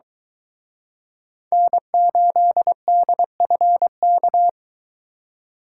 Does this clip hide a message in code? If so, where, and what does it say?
Yes: Morse code from 1.5 to 4.5 s, heard as a keyed tone, about 23 wpm, 700 Hz. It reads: N8DFK